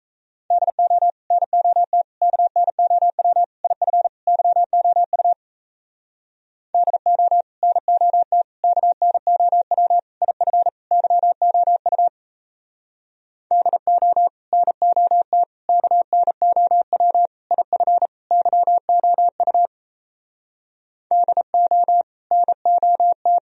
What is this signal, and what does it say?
Morse code, heard as a keyed tone, about 37 wpm, 700 Hz. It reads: DO NOT KNOW IF YOU DO NOT KNOW IF YOU DO NOT KNOW IF YOU DO NOT